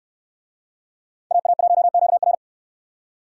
Morse code, heard as a keyed tone, about 69 wpm, 700 Hz. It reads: AA1QR